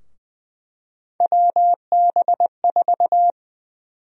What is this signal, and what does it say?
Morse code, heard as a keyed tone, about 20 wpm, 700 Hz. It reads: WB4
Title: USA Call Sign Prefixes - 20wpm